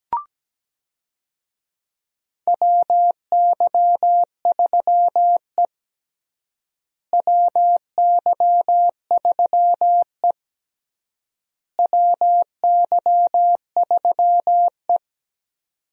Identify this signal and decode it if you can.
Morse code, heard as a keyed tone, about 17 wpm, 700 Hz. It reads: WY3E WY3E WY3E
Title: USA Call Signs - 4 Characters - Repeated 3x - 17wpm